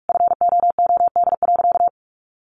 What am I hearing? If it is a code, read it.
Morse code, heard as a keyed tone, about 44 wpm, 700 Hz. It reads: FOOD.
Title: Sentences from Top 1200 Words 45wpm